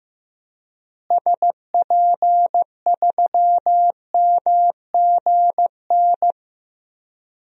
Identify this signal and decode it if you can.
Morse code, heard as a keyed tone, about 15 wpm, 700 Hz. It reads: SP3MGN
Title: DXCC-100 Call Signs - 6 Characters - Repeated 3x - 15wpm